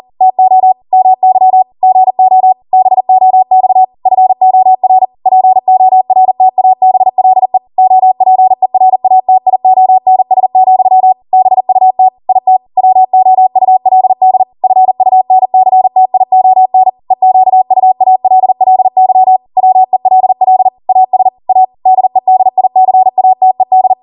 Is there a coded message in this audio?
Yes: Morse code from 0.2 to 23.9 s, heard as a keyed tone, about 40 wpm, 750 Hz. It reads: TO MY GO BOX FOR PORTABLE OPERATIONS, BUT IT WOULD FUNCTION EQUALLY WELL AS A DEDICATED